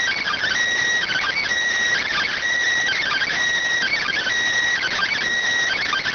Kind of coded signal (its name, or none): Emergency Alert System (SAME) tones